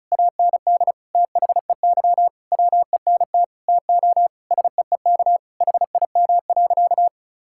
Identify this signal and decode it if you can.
Morse code, heard as a keyed tone, about 36 wpm, 700 Hz. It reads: AND THEY WENT TO SEEK HIM.
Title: Sentences from Top 1600 Words 35wpm